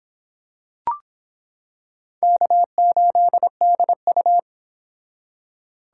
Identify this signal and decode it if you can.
Morse code, heard as a keyed tone, about 25 wpm, 700 Hz. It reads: K8DU